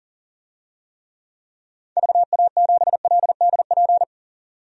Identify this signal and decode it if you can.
Morse code, heard as a keyed tone, about 40 wpm, 700 Hz. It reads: VA7LDP
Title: DXCC-100 Call Signs - 6 Characters - 40wpm